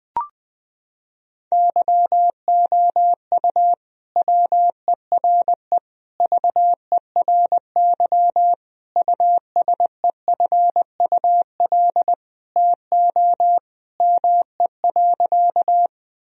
Morse code, heard as a keyed tone, about 21 wpm, 700 Hz. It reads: YOU WERE VERY USEFUL TO ME.
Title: Sentences from Top 1000 Words - Review Part 4 - 20wpm